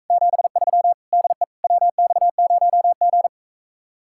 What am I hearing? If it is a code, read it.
Morse code, heard as a keyed tone, about 45 wpm, 700 Hz. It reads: 73 DE WX0G